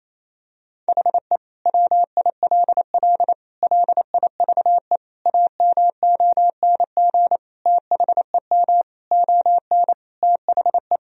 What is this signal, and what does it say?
Morse code, heard as a keyed tone, about 27 wpm, 700 Hz. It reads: HE WILL LIVE AMONG THEM ON THE
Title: Sentences from Top 1000 Words - Review Part 3 - 28wpm